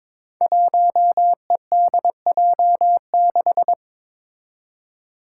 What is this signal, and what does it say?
Morse code, heard as a keyed tone, about 22 wpm, 700 Hz. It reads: 1EDJ6
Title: Five Challenging Letter-Number Combinations 22wpm